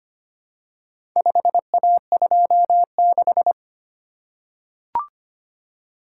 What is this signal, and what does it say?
Morse code, heard as a keyed tone, about 25 wpm, 700 Hz. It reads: 5A26
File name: practice / 25wpm / Four Letter-Number Combinations - 25wpm